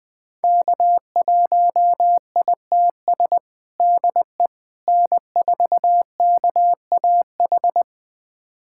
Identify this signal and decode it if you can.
Morse code, heard as a keyed tone, about 20 wpm, 700 Hz. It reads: K1ITS DE N4KAH
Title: USA Callsign DE Callsign - 5 Characters - 20wpm